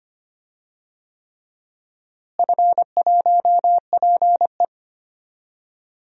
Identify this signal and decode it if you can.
Morse code, heard as a keyed tone, about 25 wpm, 700 Hz. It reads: F1PE
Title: DXCC-100 Call Signs - 4 Characters - Repeated 3x - 25wpm